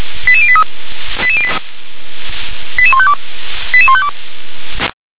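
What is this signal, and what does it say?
Five-tone selective call (ZVEI), heard as quick runs of five tones, tones of about 70 ms. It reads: 88083 88152 88152
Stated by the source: ZVEI five-tone selective call received on an SDR.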